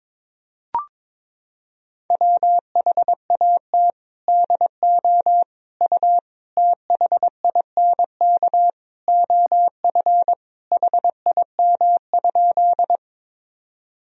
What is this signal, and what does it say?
Morse code, heard as a keyed tone, about 22 wpm, 700 Hz. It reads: WHAT DO U THINK OF HIM?
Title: Sentences from Top 200 Words and Abbreviations 22wpm